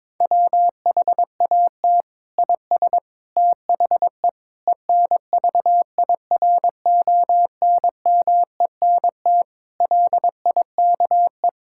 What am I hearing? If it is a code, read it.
Morse code, heard as a keyed tone, about 22 wpm, 700 Hz. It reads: WHAT IS THE ENVIRONMENT LIKE
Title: Sentences from Top 1000 Words - Review Part 5 - 22wpm